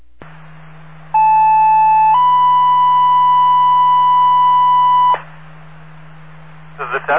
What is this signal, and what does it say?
Two-tone page, heard as a one-second tone, then a three-second tone, about 870 then 990 Hz, with speech. It is heard from 1.1 to 5.1 s.